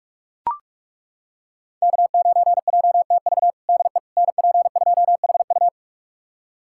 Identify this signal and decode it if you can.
Morse code, heard as a keyed tone, about 41 wpm, 700 Hz. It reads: K9JTV DE NP2HU
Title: USA Callsign DE Callsign - 5 Characters - 45wpm